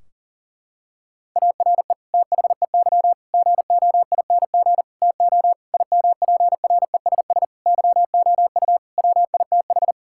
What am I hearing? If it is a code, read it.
Morse code, heard as a keyed tone, about 37 wpm, 700 Hz. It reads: ARE THEY GOING TO IMPRESS YOU WITH
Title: Sentences from Top 1600 Words 40wpm